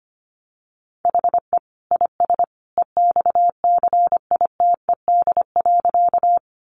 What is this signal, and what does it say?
Morse code, heard as a keyed tone, about 25 wpm, 700 Hz. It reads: HE IS EXCITED.
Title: Additional Easy Phrases 25wpm